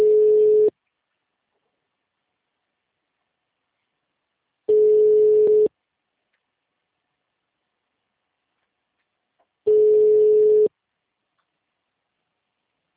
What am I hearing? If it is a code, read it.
Ringback tone (Europe, 425 Hz).
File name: out- -2008- - - .0